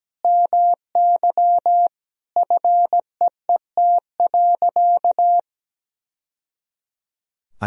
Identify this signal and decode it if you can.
Morse code, heard as a keyed tone, about 17 wpm, 700 Hz. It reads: MY FEET.
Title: Sentences from Top 1000 Words - Review Part 1 - 17wpm